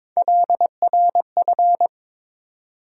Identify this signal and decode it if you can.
Morse code, heard as a keyed tone, about 21 wpm, 700 Hz. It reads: LRF
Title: USA Call Sign Postfixes - 3 Letters 22wpm